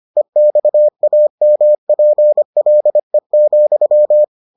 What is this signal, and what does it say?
Morse code, heard as a keyed tone, about 25 wpm, 600 Hz. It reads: EXAMPLE,